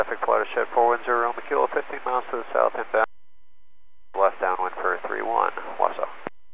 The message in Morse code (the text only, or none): none